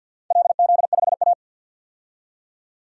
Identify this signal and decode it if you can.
Morse code, heard as a keyed tone, about 48 wpm, 700 Hz. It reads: LZ5A
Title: DXCC-100 Call Signs - 4 Characters - 50wpm